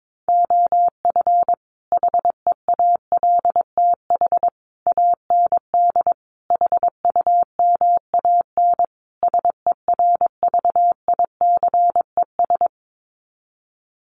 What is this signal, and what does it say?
Morse code, heard as a keyed tone, about 22 wpm, 700 Hz. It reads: OF HEALTH AND HUMAN SERVICES
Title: Sets of 5 Words 22wpm